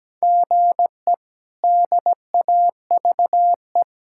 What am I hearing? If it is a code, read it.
Morse code, heard as a keyed tone, about 17 wpm, 700 Hz. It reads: GE DAVE
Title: K1USN SST QSO Practice - 17wpm